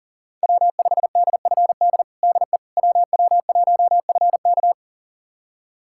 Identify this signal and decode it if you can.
Morse code, heard as a keyed tone, about 40 wpm, 700 Hz. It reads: W5DFD DE WW1FK
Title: USA Callsign DE Callsign - 5 Characters - 40wpm